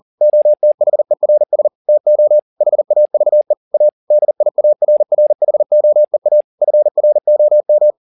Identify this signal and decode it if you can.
Morse code, heard as a keyed tone, about 40 wpm, 600 Hz. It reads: OTHERS TO HAVE A DIARRHOEA FROM